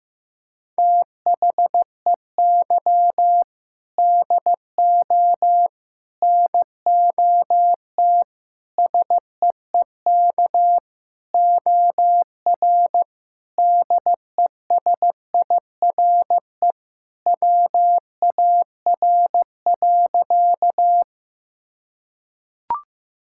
Morse code, heard as a keyed tone, about 15 wpm, 700 Hz. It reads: THEY DO NOT SEEK OR DESIRE WAR.
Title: Sentences from Top 1600 Words 15wpm